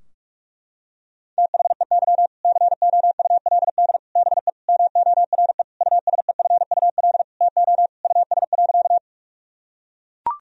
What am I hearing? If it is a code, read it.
Morse code, heard as a keyed tone, about 42 wpm, 700 Hz. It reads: THEY COULD BE MORE USEFUL TO US.
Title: Sentences from Top 1000 Words - Review Part 4 - 45wpm